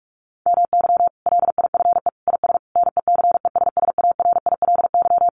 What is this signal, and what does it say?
Morse code, heard as a keyed tone, about 44 wpm, 700 Hz. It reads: MY LIFE IS NECESSARILY
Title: Sentences from Top 1400 Words 45wpm